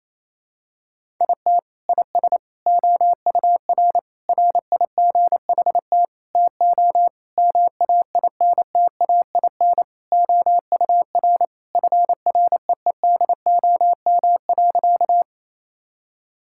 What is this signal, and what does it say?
Morse code, heard as a keyed tone, about 28 wpm, 700 Hz. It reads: IT IS OUR RIGHT TO MAINTAIN OUR FREEDOM.
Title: Sentences from Top 1000 Words - Review Part 5 - 28wpm